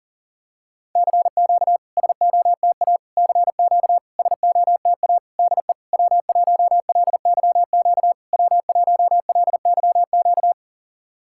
Morse code, heard as a keyed tone, about 40 wpm, 700 Hz. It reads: CQ SOTA CQ SOTA DE W1LYQ W1LYQ
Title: SOTA QSO Practice 40wpm